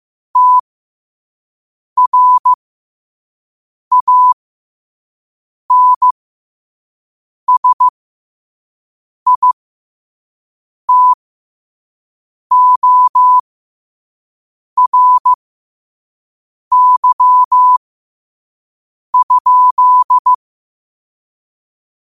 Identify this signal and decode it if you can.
Morse code, heard as a keyed tone, about 15 wpm, spaced as for 5 wpm, 1000 Hz. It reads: TRANSITORY?
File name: 5wpm